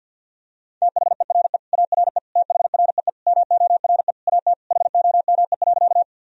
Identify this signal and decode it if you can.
Morse code, heard as a keyed tone, about 47 wpm, 700 Hz. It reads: THERE ARE THREE MORE AT HOME.
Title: Sentences from Top 1000 Words - Review Part 1 - 50wpm